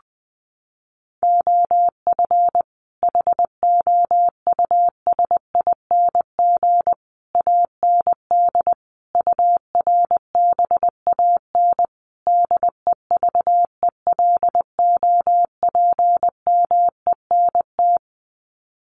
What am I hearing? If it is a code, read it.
Morse code, heard as a keyed tone, about 19 wpm, 700 Hz. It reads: OF HOUSING AND URBAN DEVELOPMENT
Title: Sets of 5 Words Encore2 20wpm